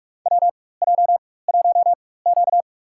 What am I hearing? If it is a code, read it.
Morse code, heard as a keyed tone, about 43 wpm, spaced as for 21 wpm, 690 Hz. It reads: WJ1Q
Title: USA Call Signs - 4x Character Spacing - 45wpm